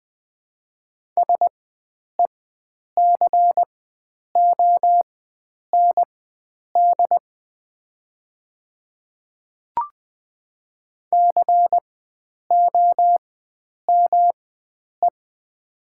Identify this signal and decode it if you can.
Morse code, heard as a keyed tone, about 20 wpm, spaced as for 9 wpm, 700 Hz. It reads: SECOND COME
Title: 4x Character Spacing - Top 300 Words - 20wpm